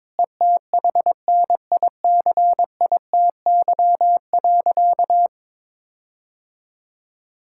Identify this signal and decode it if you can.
Morse code, heard as a keyed tone, about 22 wpm, 700 Hz. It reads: ETHNICITY.
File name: Sentences with 9-Word Phrases - 22wpm